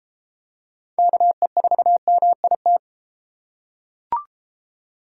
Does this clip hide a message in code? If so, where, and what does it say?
Yes: Morse code from 1.0 to 2.8 s, heard as a keyed tone, about 31 wpm, 700 Hz. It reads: KE4MIT